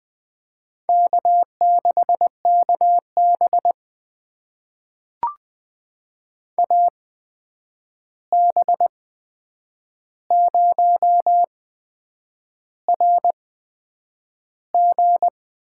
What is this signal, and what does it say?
Morse code, heard as a keyed tone, about 20 wpm, spaced as for 8 wpm, 700 Hz. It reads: K6KB AB0RG
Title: USA Call Signs - 8x Character Spacing - 20wpm